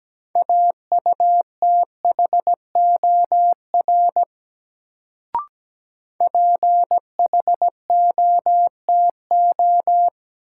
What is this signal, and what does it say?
Morse code, heard as a keyed tone, about 17 wpm, 700 Hz. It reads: AUTHOR PHOTO
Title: Sentences from Top 1200 Words - New Word Drill Alt - 17wpm